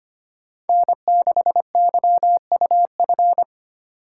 Morse code, heard as a keyed tone, about 30 wpm, 700 Hz. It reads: N6YUF
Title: Speed Racing - USA 1x3 Call Signs 25wpm